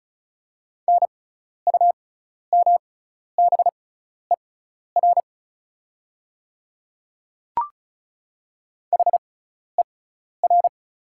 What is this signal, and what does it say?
Morse code, heard as a keyed tone, about 36 wpm, spaced as for 12 wpm, 690 Hz. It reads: NUMBER HER
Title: 6x Character Spacing - Top 100 Words - 35wpm